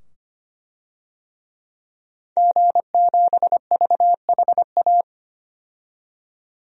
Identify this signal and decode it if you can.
Morse code, heard as a keyed tone, about 25 wpm, 700 Hz. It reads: G7VHA
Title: DXCC-100 Call Signs - 5 Characters - 25wpm